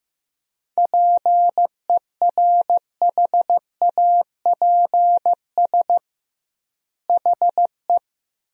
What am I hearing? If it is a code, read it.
Morse code, heard as a keyed tone, about 15 wpm, 700 Hz. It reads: PERHAPS HE
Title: 2x Word-Spacing - Sentences from Top 500 Words - 15wpm